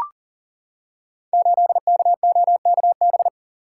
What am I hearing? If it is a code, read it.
Morse code, heard as a keyed tone, about 41 wpm, 690 Hz. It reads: 8KOKB
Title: Code Groups - 5 Letter:Numbers - 40wpm